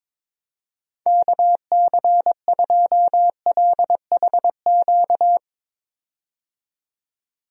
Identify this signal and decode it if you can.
Morse code, heard as a keyed tone, about 21 wpm, 700 Hz. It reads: KC2LHQ
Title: USA Call Signs - 6 Characters - 22wpm